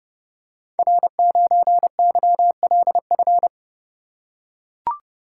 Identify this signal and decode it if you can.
Morse code, heard as a keyed tone, about 30 wpm, 700 Hz. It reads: R9YLF
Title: Code Groups - 5 Letter:Numbers - 30wpm